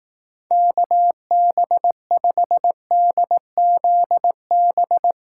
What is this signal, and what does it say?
Morse code, heard as a keyed tone, about 18 wpm, 700 Hz. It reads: KB5DZB
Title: Speed Racing - USA 2x3 Call Signs 15wpm